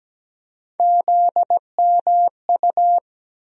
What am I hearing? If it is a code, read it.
Morse code, heard as a keyed tone, about 17 wpm, 700 Hz. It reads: ZMU
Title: USA Call Sign Postfixes - 3 Letters 17wpm